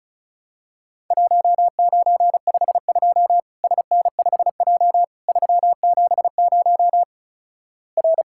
Morse code, heard as a keyed tone, about 36 wpm, 700 Hz. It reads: 1952 SN5J 370 R
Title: Worked All Europe DX Contest - QTC Exchange - 35wpm